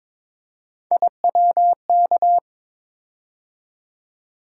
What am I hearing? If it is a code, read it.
Morse code, heard as a keyed tone, about 23 wpm, 700 Hz. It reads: IWK
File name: practice / 22wpm / Three Letter-Number Combinations - 22wpm